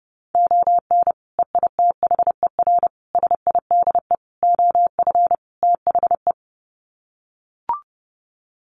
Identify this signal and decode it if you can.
Morse code, heard as a keyed tone, about 30 wpm, 700 Hz. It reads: ON EITHER SIDE OF THE
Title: Sets of 5 Words 30wpm